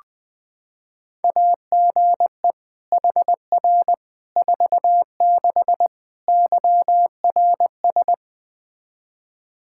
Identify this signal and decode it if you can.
Morse code, heard as a keyed tone, about 21 wpm, 700 Hz. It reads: AGE HR 46 YRS
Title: Practice QSO - Exchange 3 - More Info - 20wpm